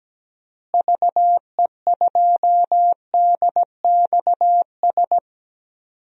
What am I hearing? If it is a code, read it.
Morse code, heard as a keyed tone, about 17 wpm, 700 Hz. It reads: VE2DXS
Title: DXCC-100 Call Signs - 6 Characters - Repeated 3x - 17wpm